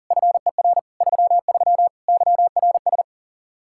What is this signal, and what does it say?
Morse code, heard as a keyed tone, about 37 wpm, 700 Hz. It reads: FER 33 YRS
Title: Practice QSO - Exchange 3 - More Info - 40wpm